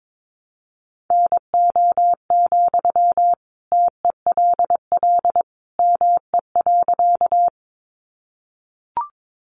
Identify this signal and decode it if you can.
Morse code, heard as a keyed tone, about 22 wpm, 700 Hz. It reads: NO, TELL ME.
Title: Sentences from Top 200 Words 22wpm